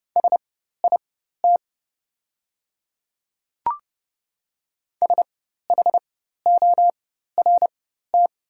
Morse code, heard as a keyed tone, about 30 wpm, spaced as for 14 wpm, 700 Hz. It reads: SIT SHORT
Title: 4x Character Spacing - Top 500 Words - 30wpm